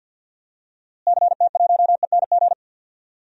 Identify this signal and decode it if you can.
Morse code, heard as a keyed tone, about 47 wpm, 700 Hz. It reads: CT1ENG
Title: DXCC-100 Call Signs - 6 Characters - Repeated 2x - 50wpm